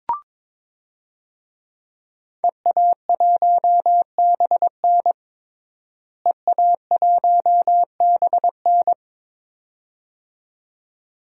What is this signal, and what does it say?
Morse code, heard as a keyed tone, about 22 wpm, 700 Hz. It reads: EA1BN EA1BN
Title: DXCC-100 Call Signs - 5 Characters - Repeated 2x - 22wpm